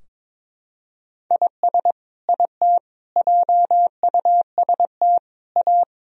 Morse code, heard as a keyed tone, about 22 wpm, 700 Hz. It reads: IS IT JUST A
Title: Sentences from Top 1000 Words - Review Part 2 - 22wpm